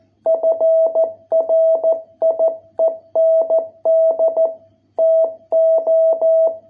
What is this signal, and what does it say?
Morse code, heard as a keyed tone, about 14 wpm, 660 Hz. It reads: FRIEND TO